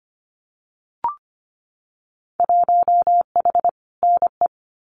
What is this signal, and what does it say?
Morse code, heard as a keyed tone, about 25 wpm, 690 Hz. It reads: 1H NE
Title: Winter Field Day Exchanges 25wpm